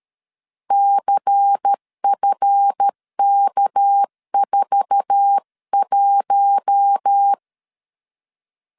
Morse code, heard as a keyed tone, about 13 wpm, 800 Hz. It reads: CFK41
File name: CW / DET / 1 9 1500C S A 12wpm 3800